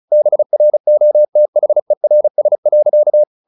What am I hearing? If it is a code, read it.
Morse code, heard as a keyed tone, about 34 wpm, 600 Hz. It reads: BROTHERS.
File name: the old man and the sea 2 35wpm